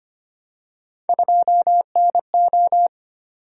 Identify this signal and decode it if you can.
Morse code, heard as a keyed tone, about 25 wpm, 700 Hz. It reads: 2NO
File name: Three Letter-Number Combinations - 25wpm